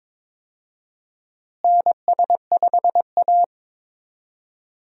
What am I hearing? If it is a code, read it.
Morse code, heard as a keyed tone, about 22 wpm, 700 Hz. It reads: NS5A
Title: USA Call Signs - 4 Characters - Repeated 3x - 22wpm